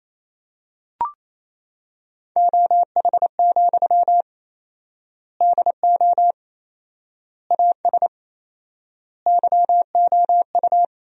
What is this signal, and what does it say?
Morse code, heard as a keyed tone, about 28 wpm, 700 Hz. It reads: OH, DO AS YOU
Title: 4x Word-Spacing - Sentences from Top 500 Words - 28wpm